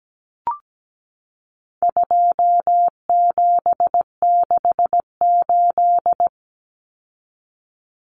Morse code, heard as a keyed tone, about 17 wpm, 700 Hz. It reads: 2768
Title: Common Cut Numbers - 4-Digit 17wpm